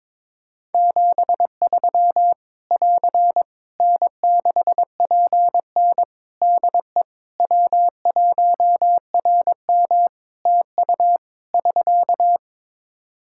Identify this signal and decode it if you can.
Morse code, heard as a keyed tone, about 22 wpm, 700 Hz. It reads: 73 <AR> N6PN DE W1RM TU <SK>
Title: Practice QSO - Exchange 4 - Ending - 22wpm